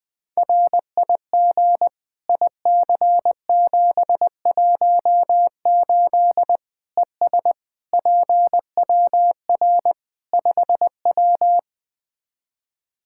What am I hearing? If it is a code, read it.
Morse code, heard as a keyed tone, about 20 wpm, 700 Hz. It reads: RIG IC718 ES PWR 5W
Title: K1USN SST QSO Practice 20wpm